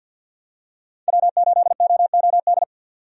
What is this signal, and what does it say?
Morse code, heard as a keyed tone, about 51 wpm, 700 Hz. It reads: W8OOD
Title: USA Call Signs - 5 Characters - 50wpm